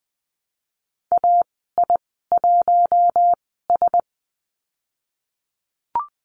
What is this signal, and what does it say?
Morse code, heard as a keyed tone, about 20 wpm, spaced as for 14 wpm, 700 Hz. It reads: AI1S
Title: 055 Lesson 10g - Letter r - Call Signs 20wpm